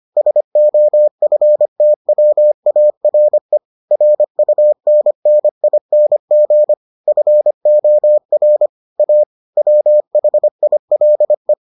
Morse code, heard as a keyed tone, about 25 wpm, 600 Hz. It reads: SOFTWARE RUNNING FOR A WHILE